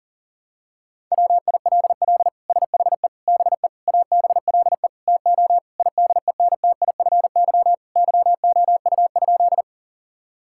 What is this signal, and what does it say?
Morse code, heard as a keyed tone, about 40 wpm, 700 Hz. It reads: WILL SHE BE ABLE TO IDENTIFY YOU?
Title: Sentences from Top 1000 Words - Review Part 6 - 40wpm